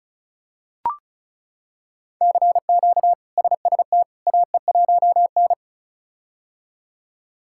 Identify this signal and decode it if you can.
Morse code, heard as a keyed tone, about 36 wpm, 700 Hz. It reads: CQ SST AE1N
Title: K1USN SST QSO Practice - 35wpm